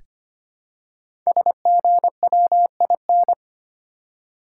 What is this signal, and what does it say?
Morse code, heard as a keyed tone, about 25 wpm, 700 Hz. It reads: SGWIN
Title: Code Groups - 5 Letters - 25wpm